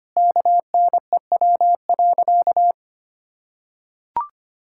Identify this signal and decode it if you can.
Morse code, heard as a keyed tone, about 25 wpm, 700 Hz. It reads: KNEW.
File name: practice / 25wpm / Sentences with 4-Word Phrases - 25wpm